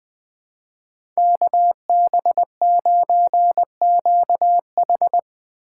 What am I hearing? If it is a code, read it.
Morse code, heard as a keyed tone, about 22 wpm, 700 Hz. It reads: KB9QH
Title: Speed Racing - USA 2x2 Call Signs 20wpm